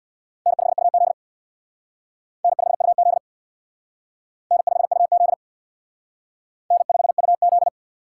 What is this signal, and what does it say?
Morse code, heard as a keyed tone, about 60 wpm, 690 Hz. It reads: N5UZ N5UZ N5UZ N5UZ
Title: Speed Racing - USA 1x2 Call Signs 50wpm